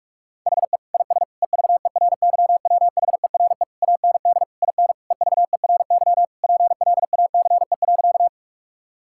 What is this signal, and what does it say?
Morse code, heard as a keyed tone, about 47 wpm, 700 Hz. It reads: HE IS EVERYWHERE AND IN EVERY PLACE.
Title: Sentences from Top 1600 Words 45wpm